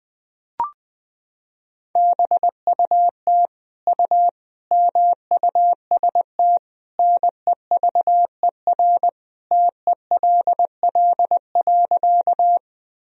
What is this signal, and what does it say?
Morse code, heard as a keyed tone, about 20 wpm, 700 Hz. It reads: BUT U MUST NEVER TELL.
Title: Sentences from Top 300 Words and Abbreviations 20wpm